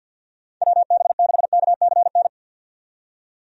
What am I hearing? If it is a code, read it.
Morse code, heard as a keyed tone, about 52 wpm, 700 Hz. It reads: WB6KCN
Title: USA Call Signs - 6 Characters - 50wpm